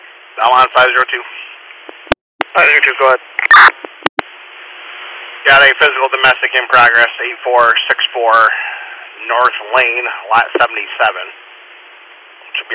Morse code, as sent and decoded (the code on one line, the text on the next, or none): none